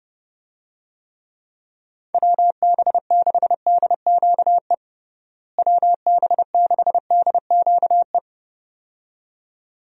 Morse code, heard as a keyed tone, about 30 wpm, 700 Hz. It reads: WB6DQE WB6DQE
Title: USA Call Signs - 6 Characters - Repeated 2x - 30wpm